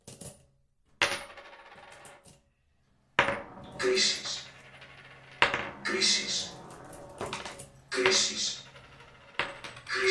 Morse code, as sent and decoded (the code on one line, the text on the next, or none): none